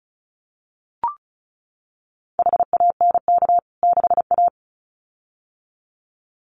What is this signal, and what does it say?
Morse code, heard as a keyed tone, about 34 wpm, 700 Hz. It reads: HANK 6A
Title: CWT Exchange 35wpm